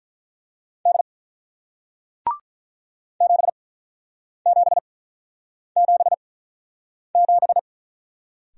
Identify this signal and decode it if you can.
Morse code, heard as a keyed tone, about 47 wpm, spaced as for 9 wpm, 700 Hz. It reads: D 7777